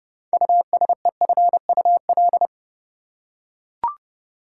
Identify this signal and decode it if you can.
Morse code, heard as a keyed tone, about 31 wpm, 700 Hz. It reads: USEFUL
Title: Six Letter Words - 30wpm